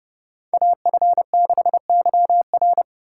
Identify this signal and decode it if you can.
Morse code, heard as a keyed tone, about 28 wpm, 700 Hz. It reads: AF6YR